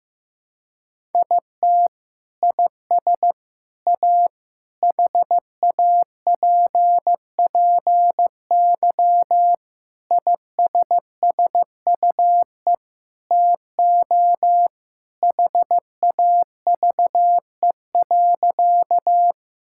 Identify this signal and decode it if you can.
Morse code, heard as a keyed tone, about 16 wpm, 700 Hz. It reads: IT IS A HAPPY ISSUE TO HAVE.
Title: Sentences from Top 1000 Words - Review Part 6 - 15wpm